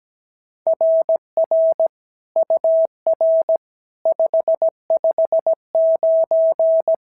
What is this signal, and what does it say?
Morse code, heard as a keyed tone, about 16 wpm, 650 Hz. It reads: RR UR 559
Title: SOTA QSO Practice 17wpm